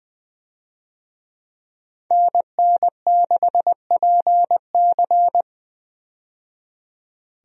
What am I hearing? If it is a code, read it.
Morse code, heard as a keyed tone, about 20 wpm, 700 Hz. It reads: NN6PC
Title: USA Call Signs - 5 Characters - 20wpm